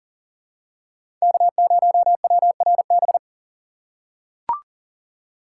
Morse code, heard as a keyed tone, about 40 wpm, 700 Hz. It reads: K0WRB
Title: USA Call Signs - 5 Characters - 4x Character Spacing - 40wpm